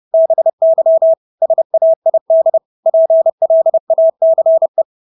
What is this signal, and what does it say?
Morse code, heard as a keyed tone, about 30 wpm, 650 Hz. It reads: BY SAID PLACE
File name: Engelske ord 30-30 wpm